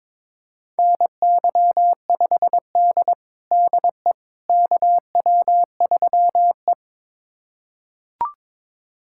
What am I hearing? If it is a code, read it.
Morse code, heard as a keyed tone, about 22 wpm, 700 Hz. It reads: NY5D DE KW3E
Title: USA Callsign DE Callsign - 4 Characters - 22wpm